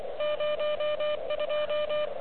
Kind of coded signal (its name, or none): Morse code